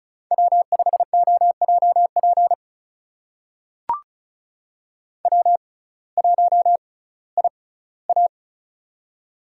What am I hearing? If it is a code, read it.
Morse code, heard as a keyed tone, about 36 wpm, spaced as for 19 wpm, 700 Hz. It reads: W5OJP W1IA
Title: USA Call Signs - 6x Character Spacing - 35wpm